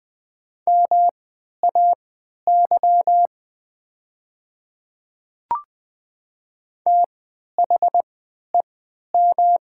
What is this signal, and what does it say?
Morse code, heard as a keyed tone, about 20 wpm, spaced as for 11 wpm, 700 Hz. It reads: MAY THEM
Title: 3x Character Spacing - Top 100 Words - 20wpm